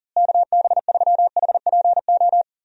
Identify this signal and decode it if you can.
Morse code, heard as a keyed tone, about 40 wpm, 700 Hz. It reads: KB3HPO
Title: USA Call Signs - 6 Characters - Repeated 3x - 40wpm